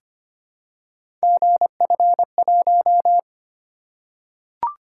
Morse code, heard as a keyed tone, about 25 wpm, 700 Hz. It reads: GF1
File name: Three Letter-Number Combinations - 25wpm